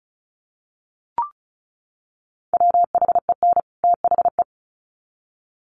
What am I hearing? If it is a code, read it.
Morse code, heard as a keyed tone, about 34 wpm, 700 Hz. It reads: WHEN THE
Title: Sets of 2 Words 35wpm